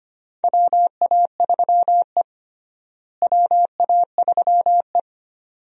Morse code, heard as a keyed tone, about 25 wpm, 700 Hz. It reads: WA3E WA3E
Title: USA Call Signs - 4 Characters - Repeated 2x - 25wpm